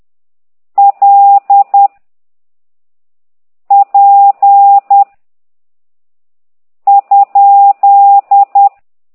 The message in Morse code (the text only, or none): L P ?